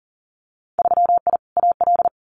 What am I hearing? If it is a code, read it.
Morse code, heard as a keyed tone, about 38 wpm, 700 Hz. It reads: 3I AL
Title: Winter Field Day Exchanges Encore 40wpm